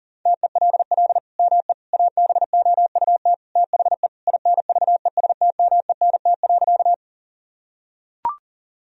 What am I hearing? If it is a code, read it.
Morse code, heard as a keyed tone, about 40 wpm, 700 Hz. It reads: TELL ME ABOUT THE INVESTMENT.
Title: Sentences from Top 1000 Words - Review Part 4 - 40wpm